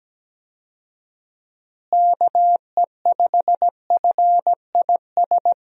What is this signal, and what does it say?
Morse code, heard as a keyed tone, about 17 wpm, 700 Hz. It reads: KE5FIS
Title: USA Call Signs - 6 Characters - Repeated 2x - 17wpm